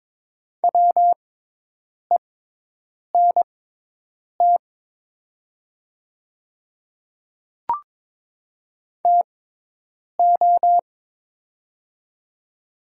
Morse code, heard as a keyed tone, about 21 wpm, spaced as for 8 wpm, 700 Hz. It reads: WENT TO
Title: 6x Character Spacing - Top 200 Words - 22wpm